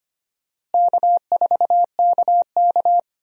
Morse code, heard as a keyed tone, about 25 wpm, 700 Hz. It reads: K4KK
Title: USA Call Signs - 4 Characters - Repeated 3x - 25wpm